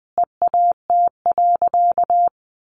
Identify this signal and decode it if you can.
Morse code, heard as a keyed tone, about 20 wpm, 700 Hz. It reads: EAT.
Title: Sentences from Top 300 Words 20wpm